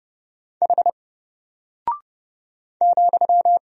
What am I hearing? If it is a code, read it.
Morse code, heard as a keyed tone, about 30 wpm, 700 Hz. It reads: H ,
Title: Single Character (Letter:Number:Punctuation:Prosign) Alt 30wpm